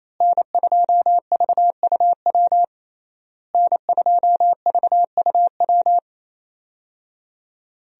Morse code, heard as a keyed tone, about 28 wpm, 700 Hz. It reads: N2VUW N2VUW
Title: USA Call Signs - 5 Characters - Repeated 3x - 28wpm